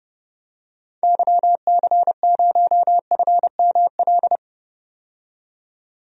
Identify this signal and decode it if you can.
Morse code, heard as a keyed tone, about 32 wpm, 690 Hz. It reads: YC0FML
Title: DXCC-100 Call Signs - 6 Characters - 30wpm